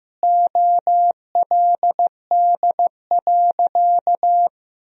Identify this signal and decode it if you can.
Morse code, heard as a keyed tone, about 15 wpm, 700 Hz. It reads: OLD.
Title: Sentences from Top 1000 Words - Review Part 1 - 15wpm